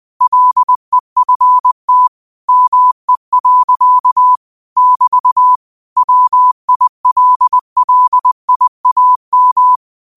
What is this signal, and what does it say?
Morse code, heard as a keyed tone, about 20 wpm, 1000 Hz. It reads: LEFT ME. <BT> WILLIAM
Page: New quotes every day in morse code at 20 Words per minute.